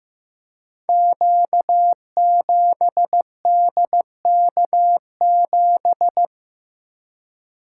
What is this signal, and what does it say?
Morse code, heard as a keyed tone, about 15 wpm, 700 Hz. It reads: Q7DK7
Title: Code Groups - 5 Letter:Numbers - 15wpm